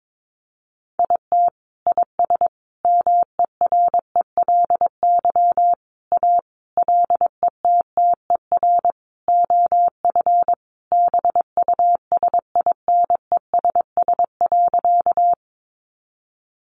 Morse code, heard as a keyed tone, about 22 wpm, 700 Hz. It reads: IT IS MERELY A LETTER OF BUSINESS.
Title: Sentences from Top 1400 Words 22wpm